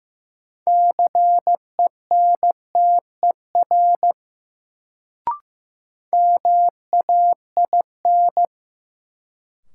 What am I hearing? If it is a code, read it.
Morse code, heard as a keyed tone, about 15 wpm, 700 Hz. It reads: CENTER MAIN
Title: Sentences from Top 1000 Words - Review Part 2 - New Word Drill - 15wpm